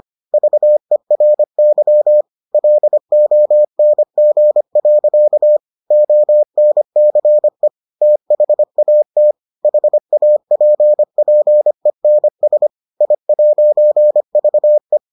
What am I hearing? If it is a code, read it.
Morse code, heard as a keyed tone, about 25 wpm, 600 Hz. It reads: VERY LONG. ONCE THAT HAPPENS I'VE